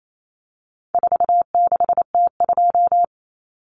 Morse code, heard as a keyed tone, about 28 wpm, 700 Hz. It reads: 46T2
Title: Common Cut Numbers - 4-Digit 28wpm